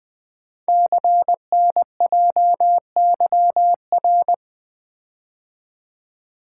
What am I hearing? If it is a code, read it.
Morse code, heard as a keyed tone, about 20 wpm, 700 Hz. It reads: CNJYR
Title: Code Groups - 5 Letters - 20wpm